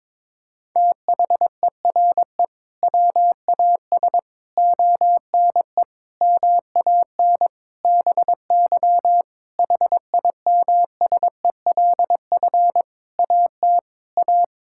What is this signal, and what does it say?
Morse code, heard as a keyed tone, about 22 wpm, 700 Hz. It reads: THERE WAS ONE MAN BY HIMSELF AT A
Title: Sentences from Top 1600 Words 22wpm